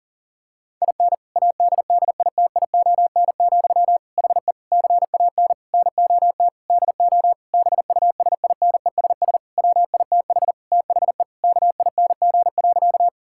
Morse code, heard as a keyed tone, about 37 wpm, 700 Hz. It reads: IN ADDITION, HE CAN NOT DO BUSINESS WITH THE KING.
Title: Sentences from Top 1000 Words - Review Part 6 - 40wpm